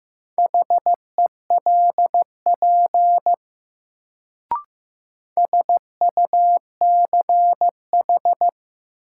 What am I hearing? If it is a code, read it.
Morse code, heard as a keyed tone, about 15 wpm, 700 Hz. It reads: HELP SUCH
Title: Sentences from Top 200 Words - New Word Drill - 15wpm